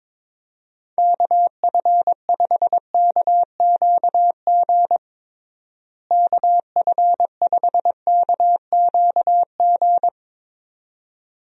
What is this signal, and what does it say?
Morse code, heard as a keyed tone, about 22 wpm, 700 Hz. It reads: KF5KQG KF5KQG
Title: USA Call Signs - 6 Characters - Repeated 2x - 22wpm